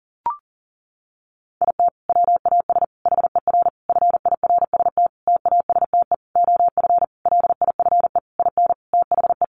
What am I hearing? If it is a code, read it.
Morse code, heard as a keyed tone, about 40 wpm, 700 Hz. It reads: IT WAS HER FIRST TASTE OF LIFE IN THE
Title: Sentences from Top 1100 Words 40wpm